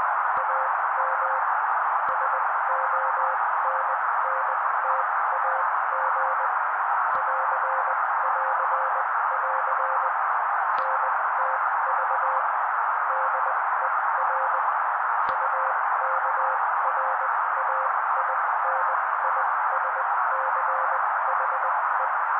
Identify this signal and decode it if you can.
Morse code, heard as a keyed tone, about 20 wpm, spaced as for 16 wpm, 530 Hz. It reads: AM SONNTAG <AR><AR><AR> NTV DER UKRAINISCHE
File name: daily / 16wpm noise